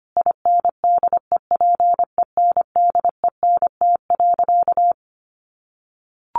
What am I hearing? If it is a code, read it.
Morse code, heard as a keyed tone, about 25 wpm, 700 Hz. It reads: INDEPENDENT.
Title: Sentences from Top 900 Words 25wpm